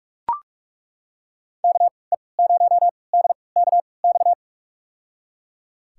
Morse code, heard as a keyed tone, about 43 wpm, spaced as for 25 wpm, 700 Hz. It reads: KE0DKX
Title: USA Call Signs - 6 Characters - 3x Character Spacing - 45wpm